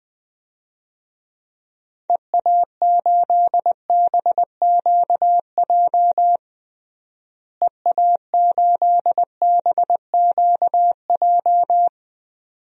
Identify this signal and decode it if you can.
Morse code, heard as a keyed tone, about 20 wpm, 700 Hz. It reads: EA8BQJ EA8BQJ
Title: DXCC-100 Call Signs - 6 Characters - Repeated 2x - 20wpm